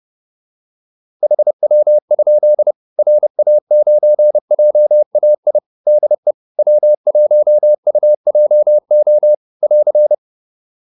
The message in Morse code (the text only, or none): HW? RA9JAI DE W1UJO <AR>